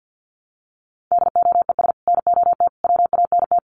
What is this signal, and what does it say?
Morse code, heard as a keyed tone, about 50 wpm, 700 Hz. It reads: DOES NOT WANT
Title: Sets of 5 Words Encore 50wpm